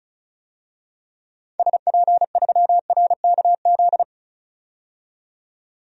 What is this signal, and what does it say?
Morse code, heard as a keyed tone, about 36 wpm, 700 Hz. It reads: SP3RKZ
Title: DXCC-100 Call Signs - 6 Characters - 35wpm